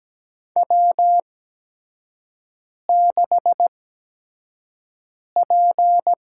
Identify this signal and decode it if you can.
Morse code, heard as a keyed tone, about 17 wpm, spaced as for 5 wpm, 700 Hz. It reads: W6P
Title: USA Call Signs - 5 Characters - 8x Character Spacing - 17wpm